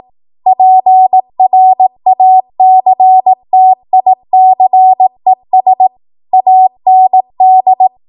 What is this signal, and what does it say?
Morse code, heard as a keyed tone, about 18 wpm, 750 Hz. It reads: PRACTICES AND
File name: Archive / 18 WPM / 18WPM